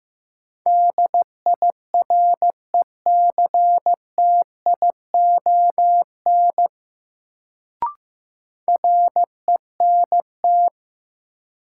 Morse code, heard as a keyed tone, about 15 wpm, 700 Hz. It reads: DIRECTION RENT
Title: Sentences from Top 1000 Words - Review Part 5 - New Word Drill - 15wpm